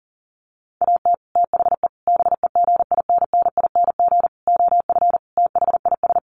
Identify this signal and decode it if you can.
Morse code, heard as a keyed tone, about 40 wpm, 700 Hz. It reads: AT THE BEGINNING OF THIS
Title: Sets of 5 Words Encore 40wpm